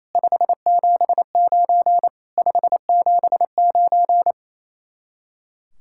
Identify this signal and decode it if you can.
Morse code, heard as a keyed tone, about 34 wpm, 700 Hz. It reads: 579 579
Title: Speed-Racing Common QSO Phrases - 28wpm